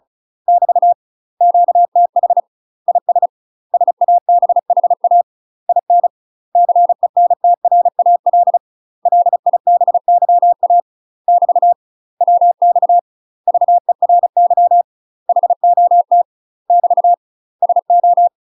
Morse code, heard as a keyed tone, about 34 wpm, 700 Hz. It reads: <BT> QTH IS SABHA IN CENTRAL LIBYA <BT> WX VERY HOT <BT> SO
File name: UK MASTER QSOs TRAINING / 35WPM / UK MASTER QSOs 0005